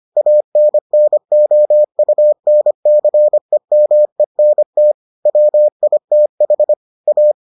Morse code, heard as a keyed tone, about 25 wpm, 600 Hz. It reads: ANNOUNCEMENT WITH A